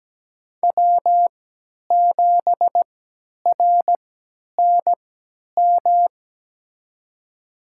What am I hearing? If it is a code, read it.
Morse code, heard as a keyed tone, about 17 wpm, spaced as for 10 wpm, 700 Hz. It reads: W7RNM
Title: USA Call Signs - 3x Character Spacing - 17wpm